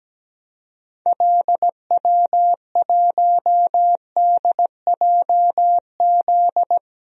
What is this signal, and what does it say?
Morse code, heard as a keyed tone, about 17 wpm, 700 Hz. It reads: LW1DJZ
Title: DXCC-100 Call Signs - 6 Characters - Repeated 2x - 17wpm